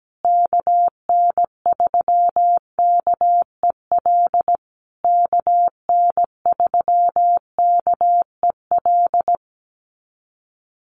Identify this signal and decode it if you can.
Morse code, heard as a keyed tone, about 17 wpm, 700 Hz. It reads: KN3KEL KN3KEL
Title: USA Call Signs Repeated 3x 17wpm